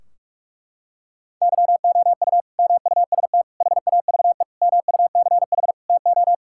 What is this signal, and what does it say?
Morse code, heard as a keyed tone, about 41 wpm, 700 Hz. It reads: YOU MUST HAVE MUCH TO
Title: Sentences from Top 1600 Words 45wpm